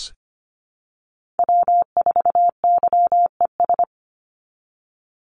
Morse code, heard as a keyed tone, about 25 wpm, 700 Hz. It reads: W4YES
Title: Call Signs Repeated 3x - 25wpm